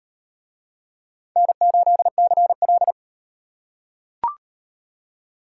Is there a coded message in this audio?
Yes: Morse code from 1.4 to 2.9 s, heard as a keyed tone, about 35 wpm, 700 Hz. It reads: N8CL